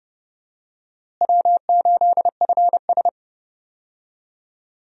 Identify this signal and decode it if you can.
Morse code, heard as a keyed tone, about 32 wpm, 700 Hz. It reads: W8FS
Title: USA Call Signs - 4 Characters - Repeated 3x - 30wpm